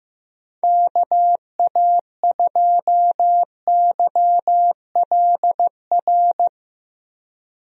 Morse code, heard as a keyed tone, about 15 wpm, 700 Hz. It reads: KA2YLR
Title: USA Call Signs - 6 Characters - Repeated 3x - 15wpm